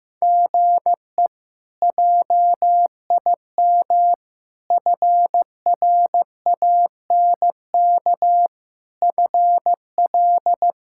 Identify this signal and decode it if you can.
Morse code, heard as a keyed tone, about 15 wpm, 700 Hz. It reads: GE JIM FRANK FL
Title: K1USN SST QSO Practice - 15wpm